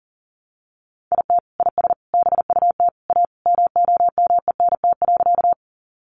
Morse code, heard as a keyed tone, about 40 wpm, 700 Hz. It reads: IT IS BUT A MOMENT.
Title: Sentences from Top 1000 Words 40wpm